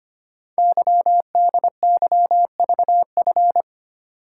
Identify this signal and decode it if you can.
Morse code, heard as a keyed tone, about 25 wpm, 700 Hz. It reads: YDYVF
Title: Code Groups - 5 Letters - 25wpm